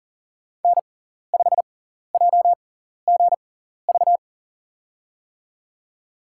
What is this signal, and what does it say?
Morse code, heard as a keyed tone, about 40 wpm, spaced as for 14 wpm, 700 Hz. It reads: N5JGV
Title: USA Call Signs - 6x Character Spacing - 40wpm